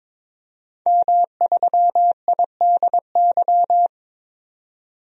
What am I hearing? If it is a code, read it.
Morse code, heard as a keyed tone, about 22 wpm, 700 Hz. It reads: M3IDY
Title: DXCC-100 Call Signs - 5 Characters - 22wpm